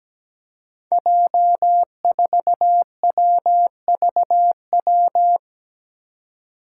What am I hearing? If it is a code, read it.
Morse code, heard as a keyed tone, about 17 wpm, 700 Hz. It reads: J4WVW
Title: Five Challenging Letter-Number Combinations 17wpm